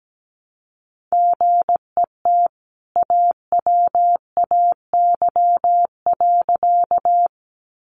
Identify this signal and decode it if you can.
Morse code, heard as a keyed tone, about 17 wpm, 700 Hz. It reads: GET AWAY.
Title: Sentences from Top 600 Words 17wpm